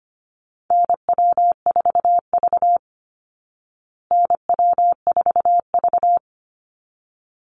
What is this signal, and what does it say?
Morse code, heard as a keyed tone, about 25 wpm, 700 Hz. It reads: NW4V NW4V
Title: Call Signs Repeated 3x - 25wpm